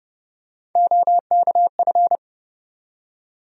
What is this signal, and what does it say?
Morse code, heard as a keyed tone, about 30 wpm, 700 Hz. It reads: OKF
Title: USA Call Sign Postfixes - 3 Letters 30wpm